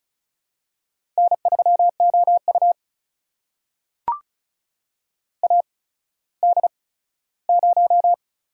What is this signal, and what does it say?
Morse code, heard as a keyed tone, about 36 wpm, spaced as for 17 wpm, 700 Hz. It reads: N3OU AD0
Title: USA Call Signs - 4 Characters - 8x Character Spacing - 35wpm